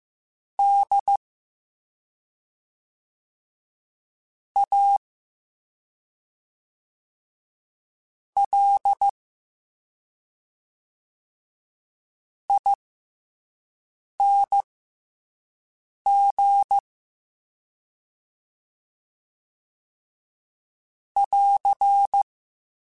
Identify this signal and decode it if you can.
Morse code, heard as a keyed tone, about 15 wpm, spaced as for 3 wpm, 800 Hz. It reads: DALING <AR>